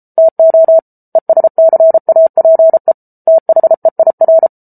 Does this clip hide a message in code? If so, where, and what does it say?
Yes: Morse code from 0.2 to 4.5 s, heard as a keyed tone, about 34 wpm, 650 Hz. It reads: TO ESCAPE THEIR